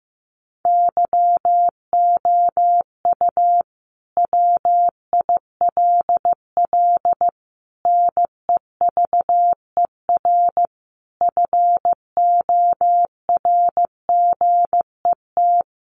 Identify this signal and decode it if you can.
Morse code, heard as a keyed tone, about 14 wpm, 700 Hz. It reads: YOU WILL NEVER FORGET
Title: Sentences from Top 1200 Words 15wpm